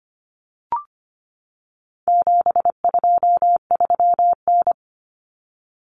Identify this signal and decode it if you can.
Morse code, heard as a keyed tone, about 25 wpm, 700 Hz. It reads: 723N
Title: Common Cut Numbers - 4-Digit 25wpm